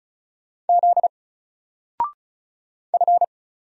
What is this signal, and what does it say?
Morse code, heard as a keyed tone, about 33 wpm, 700 Hz. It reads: Z F
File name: Single-Letter-Number-35wpm